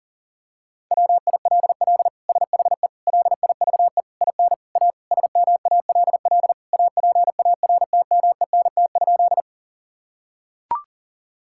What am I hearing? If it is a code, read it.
Morse code, heard as a keyed tone, about 40 wpm, 700 Hz. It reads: WILL SHE LIVE IN A SMALL APARTMENT?
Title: Sentences from Top 1000 Words - Review Part 6 - 40wpm